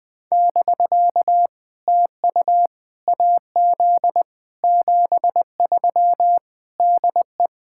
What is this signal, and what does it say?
Morse code, heard as a keyed tone, about 20 wpm, 700 Hz. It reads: <BK> TU AZ 73 DE
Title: POTA QSO Practice - 20wpm